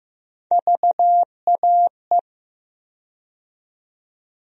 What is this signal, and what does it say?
Morse code, heard as a keyed tone, about 15 wpm, 700 Hz. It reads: VAE
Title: USA Call Sign Postfixes - 3 Letters 15wpm